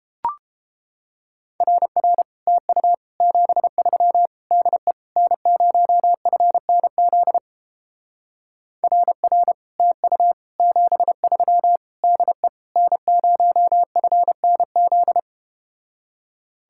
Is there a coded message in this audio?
Yes: Morse code from 1.6 to 15.2 s, heard as a keyed tone, about 32 wpm, 700 Hz. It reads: RR TU 73 DE N0FNZ RR TU 73 DE N0FNZ